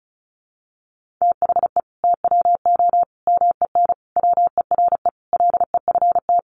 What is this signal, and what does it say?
Morse code, heard as a keyed tone, about 34 wpm, 700 Hz. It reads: THE TWO MEN WERE LEFT
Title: Sentences from Top 1000 Words 35wpm